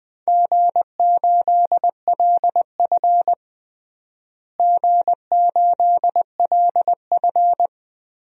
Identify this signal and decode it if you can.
Morse code, heard as a keyed tone, about 20 wpm, 700 Hz. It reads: G8LF G8LF
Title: DXCC-100 Call Signs - 4 Characters - Repeated 3x - 20wpm